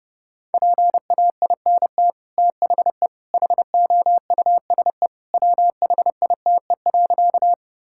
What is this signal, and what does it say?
Morse code, heard as a keyed tone, about 28 wpm, 700 Hz. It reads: PAINT THE HOUSE WHITE.
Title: Sentences from Top 1000 Words - Review Part 6 - 30wpm